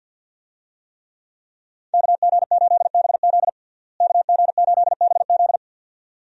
Morse code, heard as a keyed tone, about 51 wpm, 700 Hz. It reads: KG8BZ KG8BZ
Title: USA Call Signs - 5 Characters - Repeated 2x - 50wpm